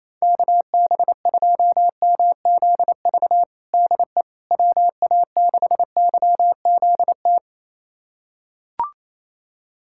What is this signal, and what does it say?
Morse code, heard as a keyed tone, about 28 wpm, 700 Hz. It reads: KB2MZV DE WA6YZT
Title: USA Callsign DE Callsign - 6 Characters - 28wpm